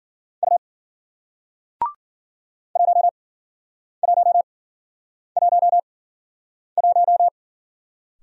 Morse code, heard as a keyed tone, about 53 wpm, spaced as for 9 wpm, 700 Hz. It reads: U 1111